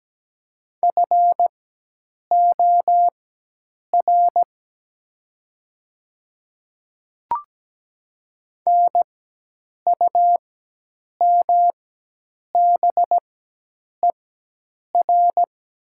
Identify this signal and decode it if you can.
Morse code, heard as a keyed tone, about 17 wpm, spaced as for 8 wpm, 700 Hz. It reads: FOR NUMBER
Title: 4x Character Spacing - Top 200 Words - 17wpm